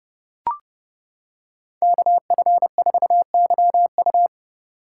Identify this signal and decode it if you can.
Morse code, heard as a keyed tone, about 30 wpm, 700 Hz. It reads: KF4YU